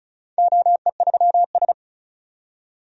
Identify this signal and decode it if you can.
Morse code, heard as a keyed tone, about 36 wpm, 700 Hz. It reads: OE3S
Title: DXCC-100 Call Signs - 4 Characters - 35wpm